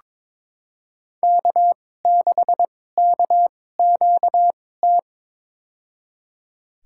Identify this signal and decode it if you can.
Morse code, heard as a keyed tone, about 22 wpm, spaced as for 16 wpm, 700 Hz. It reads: K6KQT
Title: USA Call Signs - 5 Characters - 2x Character Spacing - 22wpm